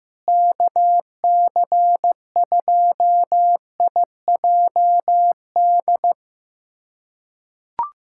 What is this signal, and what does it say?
Morse code, heard as a keyed tone, about 15 wpm, 700 Hz. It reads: KC2IJD
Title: USA Callsign DE Callsign - 6 Characters - 15wpm